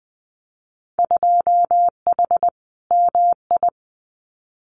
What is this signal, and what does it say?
Morse code, heard as a keyed tone, about 19 wpm, 700 Hz. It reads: 2H MI
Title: Winter Field Day Exchanges 20wpm